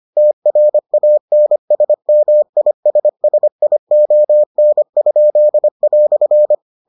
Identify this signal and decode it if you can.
Morse code, heard as a keyed tone, about 25 wpm, 600 Hz. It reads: TRANSMISSION?"